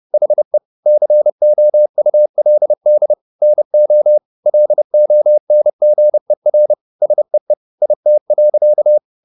Morse code, heard as a keyed tone, about 30 wpm, 600 Hz. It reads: HE COULD NO LONGER SEE IT.